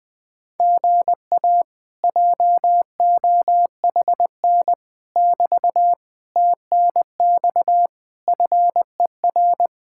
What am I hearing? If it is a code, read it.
Morse code, heard as a keyed tone, about 20 wpm, 700 Hz. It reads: GA JOHN <BT> TNX FER
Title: Speed-Racing Common QSO Phrases - 17wpm